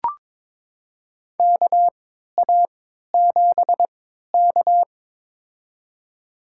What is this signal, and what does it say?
Morse code, heard as a keyed tone, about 23 wpm, spaced as for 13 wpm, 700 Hz. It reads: KA7K
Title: USA Call Signs - 4 Characters - 3x Character Spacing - 22wpm